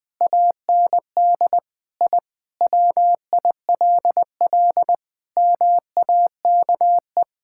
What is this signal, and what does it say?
Morse code, heard as a keyed tone, about 20 wpm, 700 Hz. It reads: AND I WILL MAKE
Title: Sentences from Top 1000 Words - Review Part 6 - 20wpm